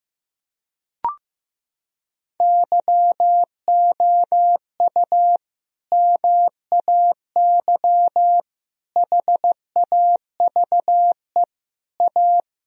Morse code, heard as a keyed tone, about 15 wpm, 700 Hz. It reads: YOU MAY HAVE A
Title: Sentences from Top 1000 Words - Review Part 2 - 15wpm